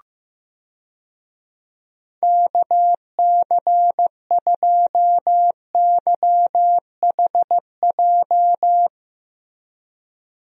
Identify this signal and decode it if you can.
Morse code, heard as a keyed tone, about 15 wpm, 700 Hz. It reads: KC2YHJ
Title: USA Call Signs - 6 Characters - Repeated 3x - 15wpm